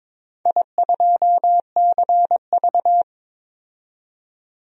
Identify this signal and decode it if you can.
Morse code, heard as a keyed tone, about 22 wpm, 700 Hz. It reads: I2CV
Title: DXCC-100 Call Signs - 4 Characters - 22wpm